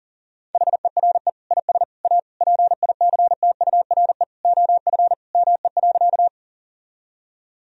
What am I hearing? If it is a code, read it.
Morse code, heard as a keyed tone, about 40 wpm, 700 Hz. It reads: HERE IS A PICTURE OF ME.
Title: Sentences from Top 1000 Words - Review Part 1 - 40wpm